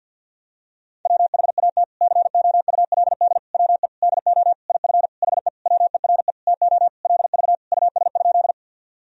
Morse code, heard as a keyed tone, about 52 wpm, 700 Hz. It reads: WHAT COULD WE DO IF HE WERE TO LV US?
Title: Sentences from Top 400 Words and Abbreviations 50wpm